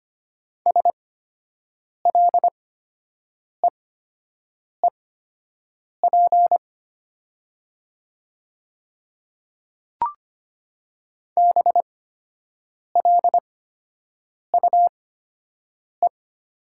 Morse code, heard as a keyed tone, about 25 wpm, spaced as for 7 wpm, 700 Hz. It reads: SLEEP BLUE
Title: 8x Character Spacing - Top 500 Words - 25wpm